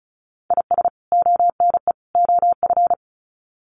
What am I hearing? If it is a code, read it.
Morse code, heard as a keyed tone, about 34 wpm, 700 Hz. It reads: IS ONE OF
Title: Sets of 3 Words 35wpm